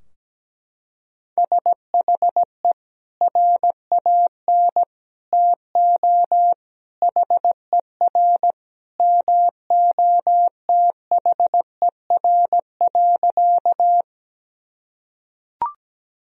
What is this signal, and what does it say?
Morse code, heard as a keyed tone, about 17 wpm, 700 Hz. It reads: SHE RAN TO HER MOTHER.
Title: Sentences from Top 1000 Words - Review Part 3 - 17wpm